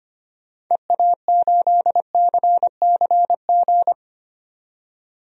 Morse code, heard as a keyed tone, about 25 wpm, 700 Hz. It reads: EA8CCG
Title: DXCC-100 Call Signs - 6 Characters - 25wpm